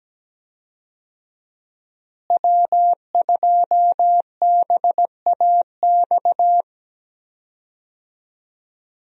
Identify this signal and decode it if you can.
Morse code, heard as a keyed tone, about 17 wpm, 700 Hz. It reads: W2BAX
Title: USA Call Signs - 5 Characters - Repeated 2x - 17wpm